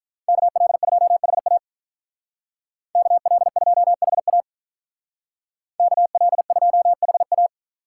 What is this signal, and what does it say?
Morse code, heard as a keyed tone, about 43 wpm, 700 Hz. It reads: KL2HA KL2HA KL2HA
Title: Speed Racing - USA 2x2 Call Signs 35wpm